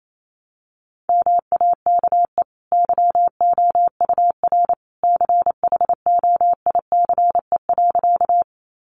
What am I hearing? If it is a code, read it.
Morse code, heard as a keyed tone, about 28 wpm, 700 Hz. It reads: MAKE YOUR CHOICE.
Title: Sentences from Top 900 Words 28wpm